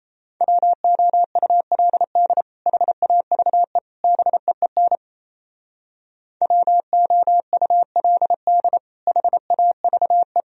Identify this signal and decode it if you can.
Morse code, heard as a keyed tone, about 33 wpm, 700 Hz. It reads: WOULD HAVE BEEN WOULD HAVE
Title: Speed-Racing - Sets of 3 Words 28wpm